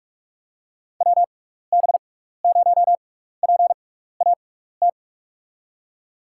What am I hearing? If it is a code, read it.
Morse code, heard as a keyed tone, about 44 wpm, spaced as for 15 wpm, 700 Hz. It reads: WB0PAT
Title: USA Call Signs - 6 Characters - 6x Character Spacing - 45wpm